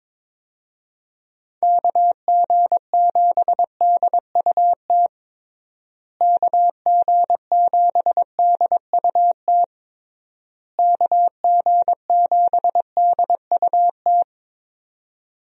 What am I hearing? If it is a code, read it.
Morse code, heard as a keyed tone, about 22 wpm, 700 Hz. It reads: KG7DUT KG7DUT KG7DUT
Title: USA Call Signs - 6 Characters - Repeated 3x - 22wpm